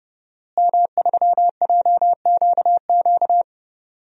Morse code, heard as a keyed tone, about 31 wpm, 700 Hz. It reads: M3JQQ
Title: DXCC-100 Call Signs - 5 Characters - Repeated 3x - 30wpm